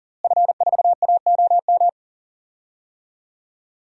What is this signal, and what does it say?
Morse code, heard as a keyed tone, about 40 wpm, 700 Hz. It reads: F4AOM
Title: DXCC-100 Call Signs - 5 Characters - Repeated 2x - 40wpm